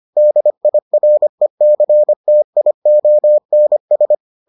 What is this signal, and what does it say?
Morse code, heard as a keyed tone, about 25 wpm, 600 Hz. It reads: DIRECTIONS